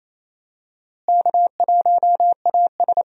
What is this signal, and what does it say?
Morse code, heard as a keyed tone, about 28 wpm, 700 Hz. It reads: K1AS
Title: USA Call Signs - 4 Characters - Repeated 3x - 28wpm